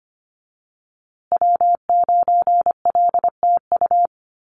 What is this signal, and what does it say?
Morse code, heard as a keyed tone, about 25 wpm, 700 Hz. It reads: W9LTU
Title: USA Call Signs Repeated 3x 25wpm